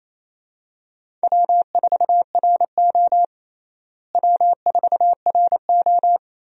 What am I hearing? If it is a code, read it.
Morse code, heard as a keyed tone, about 28 wpm, 700 Hz. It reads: W4RO W4RO
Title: USA Call Signs - 4 Characters - Repeated 2x - 28wpm